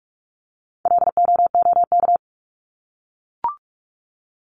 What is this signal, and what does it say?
Morse code, heard as a keyed tone, about 44 wpm, 700 Hz. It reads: LOOK
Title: ING Suffix - Top 100 Words with Base Words 45wpm